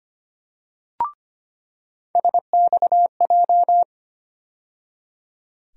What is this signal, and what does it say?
Morse code, heard as a keyed tone, about 25 wpm, 700 Hz. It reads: SXJ
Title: USA Call Sign Postfixes - 3 Letters 25wpm